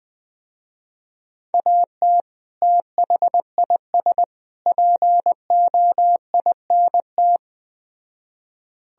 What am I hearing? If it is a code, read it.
Morse code, heard as a keyed tone, about 21 wpm, 700 Hz. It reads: AT THIS POINT
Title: Speed-Racing - Sets of 3 Words 17wpm